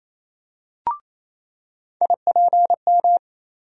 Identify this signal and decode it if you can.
Morse code, heard as a keyed tone, about 28 wpm, 700 Hz. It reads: IPM
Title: USA Call Sign Postfixes - 3 Letters 28wpm